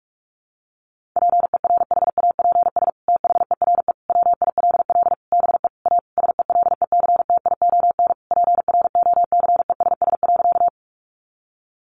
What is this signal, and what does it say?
Morse code, heard as a keyed tone, about 44 wpm, 700 Hz. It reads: PERHAPS THERE WILL BE A SELECTION PROCESS.
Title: Sentences from Top 1400 Words 45wpm